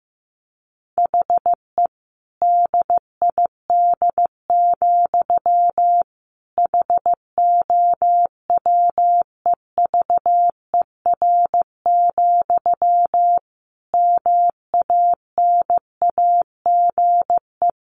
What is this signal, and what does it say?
Morse code, heard as a keyed tone, about 15 wpm, 700 Hz. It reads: HE DID, HOWEVER, MANAGE
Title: Sentences from Top 1400 Words 15wpm